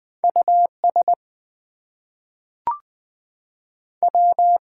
Morse code, heard as a keyed tone, about 20 wpm, 700 Hz. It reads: US W
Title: Two Letter Words - 20wpm